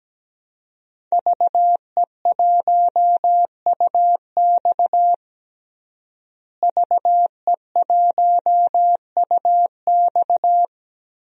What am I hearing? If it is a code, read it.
Morse code, heard as a keyed tone, about 17 wpm, 700 Hz. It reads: VE1UX VE1UX
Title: DXCC-100 Call Signs - 5 Characters - Repeated 3x - 17wpm